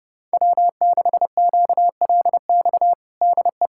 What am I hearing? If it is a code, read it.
Morse code, heard as a keyed tone, about 32 wpm, 700 Hz. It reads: W6QLX DE
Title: USA Callsign DE Callsign - 5 Characters - 30wpm